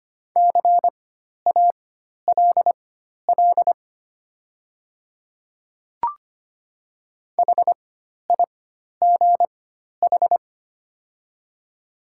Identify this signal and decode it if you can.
Morse code, heard as a keyed tone, about 25 wpm, spaced as for 12 wpm, 690 Hz. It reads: CALL HIGH
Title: 4x Character Spacing - Top 500 Words - 25wpm